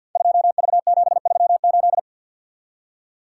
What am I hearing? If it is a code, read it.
Morse code, heard as a keyed tone, about 46 wpm, 700 Hz. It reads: 2V738
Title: Code Groups - 5 Letter:Numbers - 50wpm